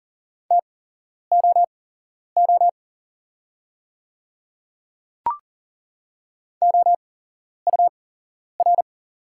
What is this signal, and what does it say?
Morse code, heard as a keyed tone, about 40 wpm, spaced as for 11 wpm, 700 Hz. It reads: TOO OUR
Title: 8x Character Spacing - Top 300 Words - 40wpm